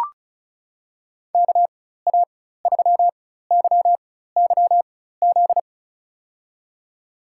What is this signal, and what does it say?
Morse code, heard as a keyed tone, about 36 wpm, spaced as for 16 wpm, 700 Hz. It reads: KA3YYZ
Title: USA Call Signs - 6 Characters - 4x Character Spacing - 35wpm